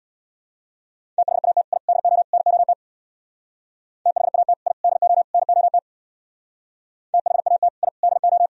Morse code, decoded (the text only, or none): THAT I DO NOT THAT I DO NOT THAT I DO